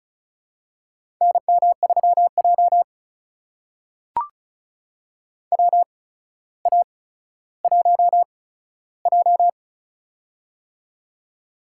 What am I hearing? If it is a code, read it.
Morse code, heard as a keyed tone, about 36 wpm, spaced as for 15 wpm, 700 Hz. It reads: NM3J WA1J
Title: USA Call Signs - 4 Characters - 8x Character Spacing - 35wpm